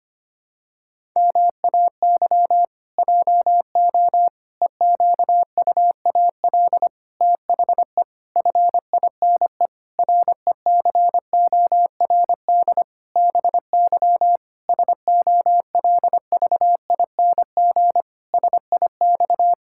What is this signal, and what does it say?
Morse code, heard as a keyed tone, about 25 wpm, 700 Hz. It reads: MAY JO EQUAL THE FINE RECORD BY SOLVING SIX
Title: Pangram Challenge v2 - 25wpm